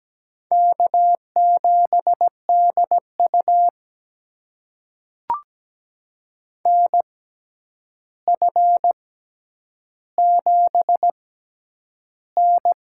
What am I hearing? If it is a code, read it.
Morse code, heard as a keyed tone, about 17 wpm, spaced as for 9 wpm, 700 Hz. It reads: K7DU NF7N
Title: USA Call Signs - 4 Characters - 6x Character Spacing - 17wpm